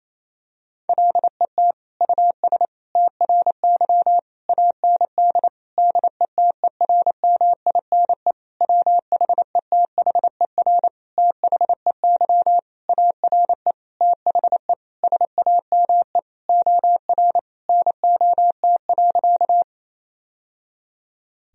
Morse code, heard as a keyed tone, about 28 wpm, 700 Hz. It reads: LET US TRY AND DETERMINE WHETHER THEY ARE THE SAME OR NOT.
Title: Sentences from Top 1000 Words - Review Part 4 - 28wpm